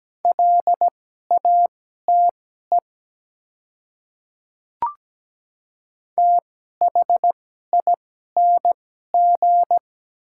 Morse code, heard as a keyed tone, about 17 wpm, spaced as for 12 wpm, 700 Hz. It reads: LATE THING
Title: 2x Character Spacing - Top 300 Words - 17wpm